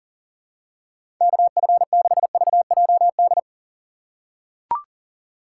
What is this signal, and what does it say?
Morse code, heard as a keyed tone, about 40 wpm, 700 Hz. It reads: KF6VJD
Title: USA Call Signs - 6 Characters - 6x Character Spacing - 40wpm